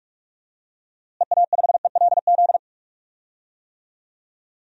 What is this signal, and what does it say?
Morse code, heard as a keyed tone, about 46 wpm, 700 Hz. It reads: EA5ELZ
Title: DXCC-100 Call Signs - 6 Characters - Repeated 3x - 45wpm